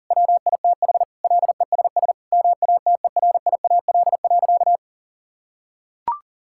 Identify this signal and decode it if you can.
Morse code, heard as a keyed tone, about 40 wpm, 700 Hz. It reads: WITH LESS MATERIAL.
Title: Sentences from Top 1000 Words - Review Part 5 - 40wpm